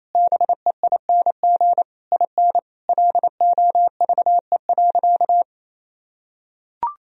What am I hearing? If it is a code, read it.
Morse code, heard as a keyed tone, about 28 wpm, 700 Hz. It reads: BEING IN LOVE.
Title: Sentences from Top 1000 Words - Review Part 3 - 28wpm